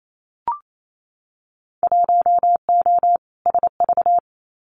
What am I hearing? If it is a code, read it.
Morse code, heard as a keyed tone, about 28 wpm, 700 Hz. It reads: 1O SV
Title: Winter Field Day Exchanges 28wpm